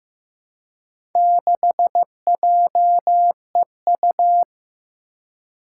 Morse code, heard as a keyed tone, about 15 wpm, 700 Hz. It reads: 6JEU
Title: Four Challenging Letter-Number Combinations 15wpm